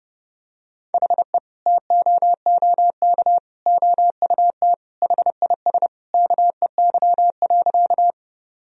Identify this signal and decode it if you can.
Morse code, heard as a keyed tone, about 30 wpm, 700 Hz. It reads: HE TOOK OUT HIS KEY.
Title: Sentences from Top 1000 Words - Review Part 4 - 30wpm